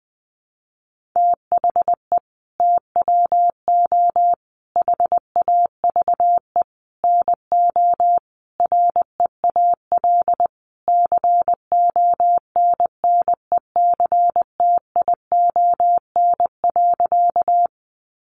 Morse code, heard as a keyed tone, about 20 wpm, 700 Hz. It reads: THE TWO HAVE NO REAL CONNECTION.
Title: Sentences from Top 1300 Words 20wpm